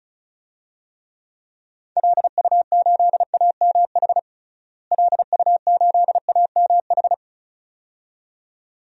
Morse code, heard as a keyed tone, about 34 wpm, 700 Hz. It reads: LU8AMH LU8AMH
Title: DXCC-100 Call Signs - 6 Characters - Repeated 2x - 35wpm